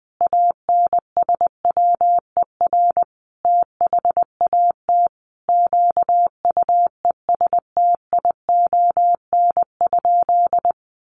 Morse code, heard as a keyed tone, about 20 wpm, 700 Hz. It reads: ANSWER THAT QUESTION?
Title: Sentences from Top 400 Words 20wpm